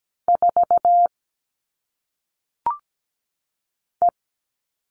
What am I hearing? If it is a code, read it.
Morse code, heard as a keyed tone, about 17 wpm, 700 Hz. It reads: 4 E
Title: All Cut Numbers - 1-Digit 17wpm